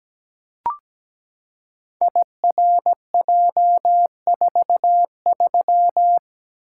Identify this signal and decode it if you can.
Morse code, heard as a keyed tone, about 17 wpm, 700 Hz. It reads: IRJ43
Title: Code Groups - 5 Letter:Numbers - 17wpm